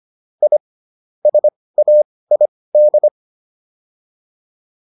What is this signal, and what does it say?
Morse code, heard as a keyed tone, about 25 wpm, spaced as for 18 wpm, 600 Hz. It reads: I SAID
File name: WD104 18